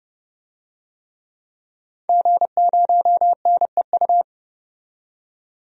Morse code, heard as a keyed tone, about 30 wpm, 700 Hz. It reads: G0NEU
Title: DXCC-100 Call Signs - 5 Characters - 30wpm